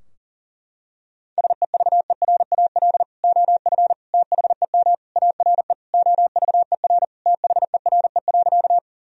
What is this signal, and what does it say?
Morse code, heard as a keyed tone, about 40 wpm, 700 Hz. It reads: SEVERAL OF THEM ARE OVER THERE.
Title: Sentences from Top 1000 Words - Review Part 2 - 40wpm